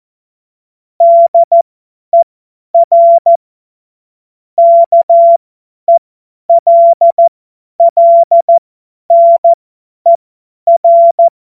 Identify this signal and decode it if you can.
Morse code, heard as a keyed tone, about 14 wpm, spaced as for 10 wpm, 680 Hz. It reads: DER KELLNER
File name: Morsekurs / Witze / Witz 11. Mai 2021, 50 F70